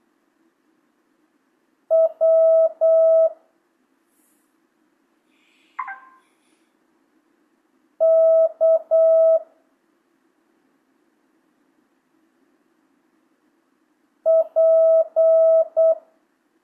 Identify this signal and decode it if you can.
Morse code, heard as a keyed tone, about 8 wpm, 650 Hz. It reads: W K P
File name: jh8cbh / 2025cw